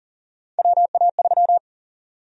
Morse code, heard as a keyed tone, about 40 wpm, 700 Hz. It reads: WA3
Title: USA Call Sign Prefixes - Repeated 2x - 40wpm